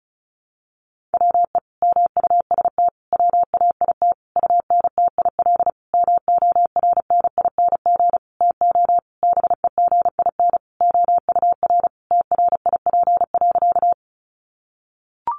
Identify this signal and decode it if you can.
Morse code, heard as a keyed tone, about 34 wpm, 700 Hz. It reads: WE MUST WAIT UNTIL MORNING TO BEGIN OUR TRIP.
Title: Sentences from Top 1300 Words 35wpm